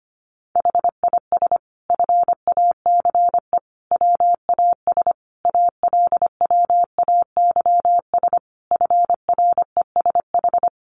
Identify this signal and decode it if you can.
Morse code, heard as a keyed tone, about 25 wpm, 700 Hz. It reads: HIS FACE WAS ALWAYS FRESH
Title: Sentences from Top 1200 Words 25wpm